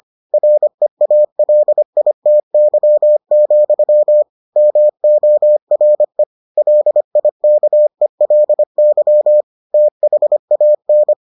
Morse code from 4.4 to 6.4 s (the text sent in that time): MORE